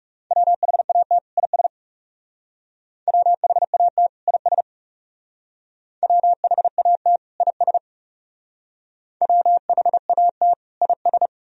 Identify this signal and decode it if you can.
Morse code, heard as a keyed tone, about 36 wpm, 700 Hz. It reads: WHAT IS WHAT IS WHAT IS WHAT IS
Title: Speed-Racing - Sets of 2 Words 30wpm